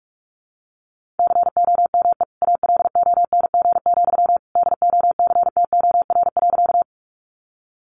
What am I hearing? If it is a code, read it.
Morse code, heard as a keyed tone, about 44 wpm, 700 Hz. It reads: COME ALONG, DOCTOR.
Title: Sentences from Top 1500 Words 45wpm